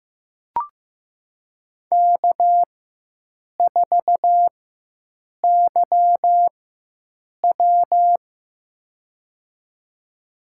Morse code, heard as a keyed tone, about 15 wpm, spaced as for 7 wpm, 700 Hz. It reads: K4YW
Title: USA Call Signs - 4 Characters - 4x Character Spacing - 15wpm